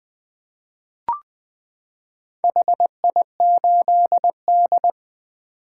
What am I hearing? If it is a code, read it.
Morse code, heard as a keyed tone, about 20 wpm, 700 Hz. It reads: HI8D
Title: 2023 POTA Activator Call Signs - 20wpm